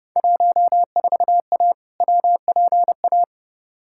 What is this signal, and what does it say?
Morse code, heard as a keyed tone, about 30 wpm, 700 Hz. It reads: 14A WPA
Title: Field Day Exchanges Encore - 30wpm